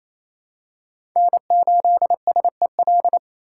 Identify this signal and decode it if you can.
Morse code, heard as a keyed tone, about 28 wpm, 700 Hz. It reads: N8SEL
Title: USA Call Signs - 5 Characters - 2x Character Spacing - 28wpm